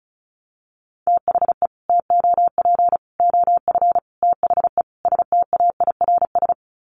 Morse code, heard as a keyed tone, about 34 wpm, 700 Hz. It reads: THE TOP OF THE STAIRS
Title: Sets of 5 Words Encore 35wpm